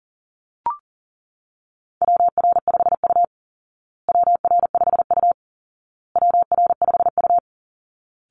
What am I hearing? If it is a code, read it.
Morse code, heard as a keyed tone, about 37 wpm, 700 Hz. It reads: WR5U WR5U WR5U
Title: Call Signs Repeated 2x - 40wpm